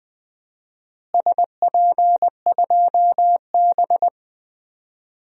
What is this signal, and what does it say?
Morse code, heard as a keyed tone, about 20 wpm, 700 Hz. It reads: SP2B
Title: DXCC-100 Call Signs - 4 Characters - Repeated 2x - 20wpm